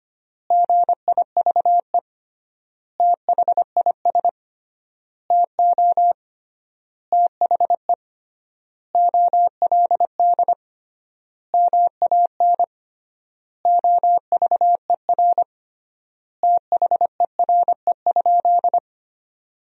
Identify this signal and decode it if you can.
Morse code, heard as a keyed tone, about 25 wpm, 700 Hz. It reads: GIVE THIS TO THE OLD MAN OVER THERE?
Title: 3x Word-Spacing - Sentences from Top 200 Words - 25wpm